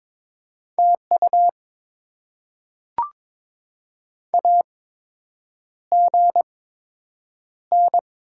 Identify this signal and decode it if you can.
Morse code, heard as a keyed tone, about 22 wpm, spaced as for 8 wpm, 700 Hz. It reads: TU AGN
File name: Top 20 QSO Elements - 8x Character Spacing - 22wpm